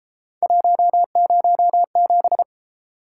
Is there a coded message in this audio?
Yes: Morse code from 0.4 to 2.4 s, heard as a keyed tone, about 33 wpm, 700 Hz. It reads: 107